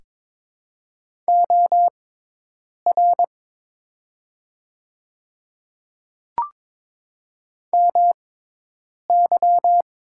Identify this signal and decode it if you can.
Morse code, heard as a keyed tone, about 22 wpm, spaced as for 8 wpm, 700 Hz. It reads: OR MY
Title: 6x Character Spacing - Top 500 Words - 22wpm